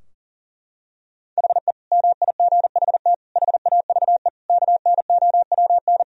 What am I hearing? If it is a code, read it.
Morse code, heard as a keyed tone, about 40 wpm, 700 Hz. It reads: HE MIGHT HAVE KNOWN
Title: Sentences from Top 1000 Words - Review Part 3 - 40wpm